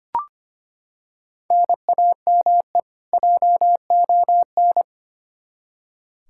Morse code, heard as a keyed tone, about 25 wpm, 700 Hz. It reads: NAME JON
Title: K1USN SST QSO Practice 25wpm